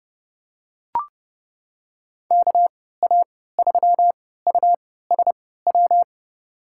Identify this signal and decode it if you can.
Morse code, heard as a keyed tone, about 31 wpm, spaced as for 17 wpm, 700 Hz. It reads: KA3USW
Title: USA Call Signs - 6 Characters - 3x Character Spacing - 30wpm